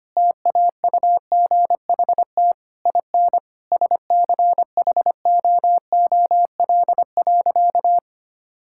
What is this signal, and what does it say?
Morse code, heard as a keyed tone, about 25 wpm, 700 Hz. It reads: TAUGHT IN SCHOOL.
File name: practice / 25wpm / Difficult Sentences with Hints - 25wpm